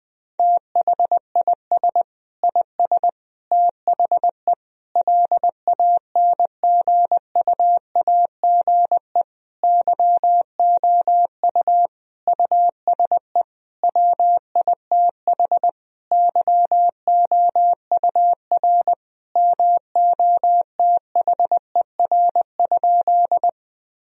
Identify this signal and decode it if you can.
Morse code, heard as a keyed tone, about 20 wpm, 700 Hz. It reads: THIS IS THE LANGUAGE YOU USE WITH YOUR MOTHER?
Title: Sentences from Top 1000 Words - Review Part 3 - 20wpm